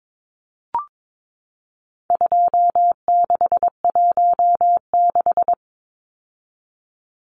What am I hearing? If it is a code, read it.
Morse code, heard as a keyed tone, about 22 wpm, 700 Hz. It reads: 2616
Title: Numbers - 4-Digit 22wpm